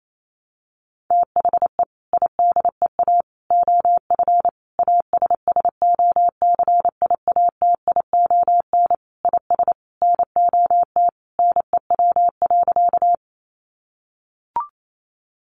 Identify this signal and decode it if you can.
Morse code, heard as a keyed tone, about 28 wpm, 700 Hz. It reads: THE IDEA OF ASSOCIATION IS NOT NEW.
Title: Sentences from Top 1100 Words 28wpm